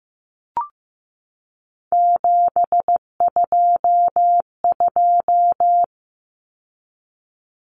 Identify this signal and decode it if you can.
Morse code, heard as a keyed tone, about 15 wpm, 700 Hz. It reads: 722
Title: Numbers - 3-Digit 15wpm